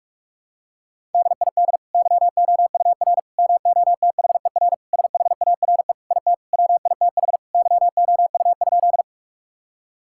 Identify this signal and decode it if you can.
Morse code, heard as a keyed tone, about 46 wpm, 700 Hz. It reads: DID YOUR MOTHER SHARE IT WITH YOU?
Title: Sentences from Top 1000 Words - Review Part 6 - 45wpm